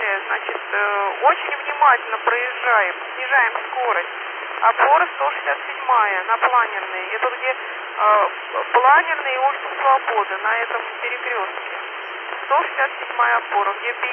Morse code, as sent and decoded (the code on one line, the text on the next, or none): none